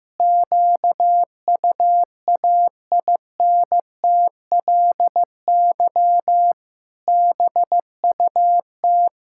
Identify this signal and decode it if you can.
Morse code, heard as a keyed tone, about 15 wpm, 700 Hz. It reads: QUAINTLY BUT
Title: Pangram Challenge v2 - 15wpm